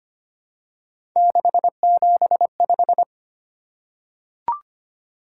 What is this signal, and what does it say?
Morse code, heard as a keyed tone, about 25 wpm, 700 Hz. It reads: 675
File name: Three Letter-Number Combinations - 25wpm